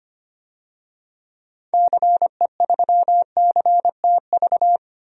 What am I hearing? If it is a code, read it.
Morse code, heard as a keyed tone, about 25 wpm, 700 Hz. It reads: CE3CTV
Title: DXCC-100 Call Signs - 6 Characters - 25wpm